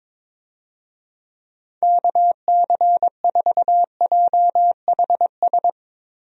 Morse code, heard as a keyed tone, about 22 wpm, 700 Hz. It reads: KC4JHS
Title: USA Call Signs - 6 Characters - 22wpm